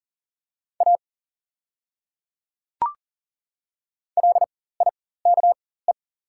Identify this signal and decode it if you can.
Morse code, heard as a keyed tone, about 38 wpm, spaced as for 18 wpm, 690 Hz. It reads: A LIKE
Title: 4x Character Spacing - Top 100 Words - 40wpm